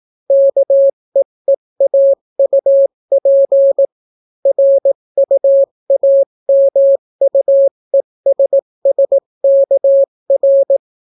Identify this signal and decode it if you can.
Morse code, heard as a keyed tone, about 18 wpm, 550 Hz. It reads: KEEAUP RUAMUESSKR